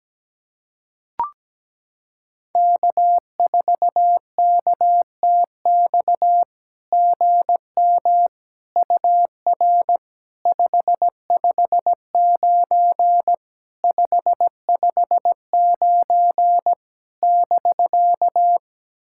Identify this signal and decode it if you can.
Morse code, heard as a keyed tone, about 17 wpm, 700 Hz. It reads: K4KTX GM UR 559 559 <BK>
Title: SOTA QSO Practice 17wpm